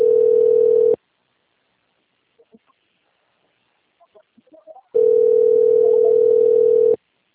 Ringback tone (North America), 440 + 480 Hz.
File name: rg-1600-unknown- - - .33942